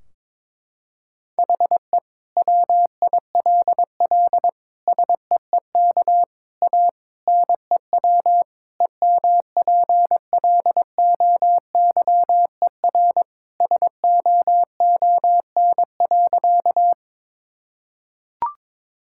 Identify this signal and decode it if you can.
Morse code, heard as a keyed tone, about 22 wpm, 700 Hz. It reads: HE WILL SEEK A NEW EMPLOYER SOON.
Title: Sentences from Top 1600 Words 22wpm